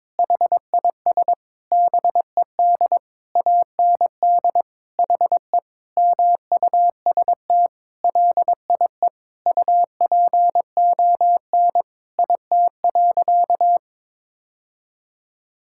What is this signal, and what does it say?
Morse code, heard as a keyed tone, about 22 wpm, 700 Hz. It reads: HIS BED AND HE MUST LIE UPON IT.
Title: Sentences from Top 1600 Words 22wpm